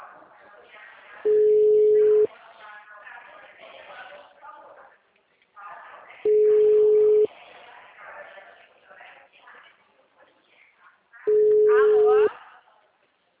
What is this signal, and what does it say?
Ringback tone (Europe, 425 Hz), with speech.
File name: out- -2030- - - .26357